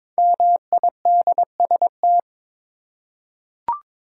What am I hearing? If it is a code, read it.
Morse code, heard as a keyed tone, about 22 wpm, 690 Hz. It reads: MIDST
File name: 5 Letter Words - First 3 Letters Given - 22wpm